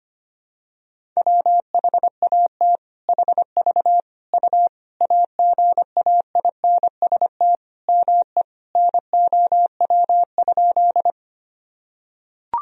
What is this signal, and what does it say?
Morse code, heard as a keyed tone, about 25 wpm, 700 Hz. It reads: WHAT HV U AGAINST ME NOW?
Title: Sentences from Top 400 Words and Abbreviations 25wpm